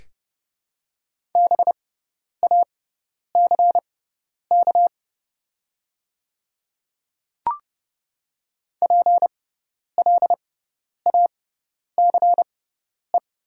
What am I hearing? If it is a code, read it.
Morse code, heard as a keyed tone, about 30 wpm, spaced as for 10 wpm, 690 Hz. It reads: BACK PLACE
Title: 6x Character Spacing - Top 200 Words - 30wpm